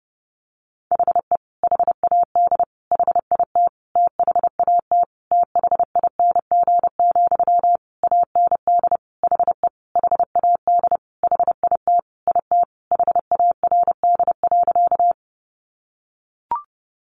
Morse code, heard as a keyed tone, about 30 wpm, 700 Hz. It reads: HE HAD HIT THAT THING, AND HE HAD HIT IT HARD.
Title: Sentences from Top 1200 Words 30wpm